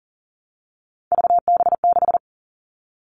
Morse code, heard as a keyed tone, about 40 wpm, 700 Hz. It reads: VB6
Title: Three Challenging Letter-Number Combinations 40wpm